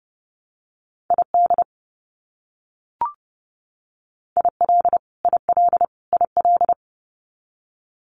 Morse code, heard as a keyed tone, about 30 wpm, 700 Hz. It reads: ID IL IL IL
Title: US States Repeated 3x 30wpm